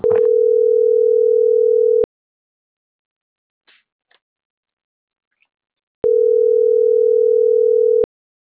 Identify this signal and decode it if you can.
Ringback tone (North America), 440 + 480 Hz.